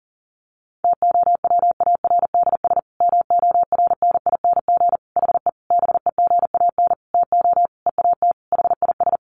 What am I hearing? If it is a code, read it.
Morse code, heard as a keyed tone, about 40 wpm, 700 Hz. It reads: TOWARDS MORNING HE BEGAN TO EAT HIS
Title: Sentences from Top 1100 Words 40wpm